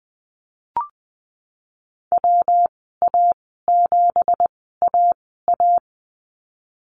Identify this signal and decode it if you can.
Morse code, heard as a keyed tone, about 20 wpm, spaced as for 14 wpm, 700 Hz. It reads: WA7AA
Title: 173 Lesson 28g - Number 9 - Call Signs 20wpm